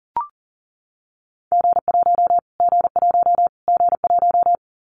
Morse code, heard as a keyed tone, about 40 wpm, 700 Hz. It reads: G1 G1 G1
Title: Callsign Prefixes for Top 25 DXCC - Repeated 3x - 40wpm